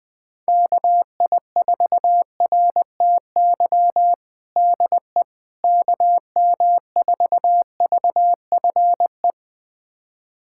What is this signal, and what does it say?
Morse code, heard as a keyed tone, about 20 wpm, 700 Hz. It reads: KI4RTY DE KM4VFE
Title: USA Callsign DE Callsign - 6 Characters - 20wpm